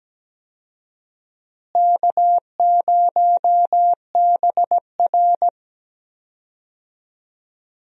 Morse code, heard as a keyed tone, about 17 wpm, 700 Hz. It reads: K0BR
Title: USA Call Signs - 4 Characters - Repeated 2x - 17wpm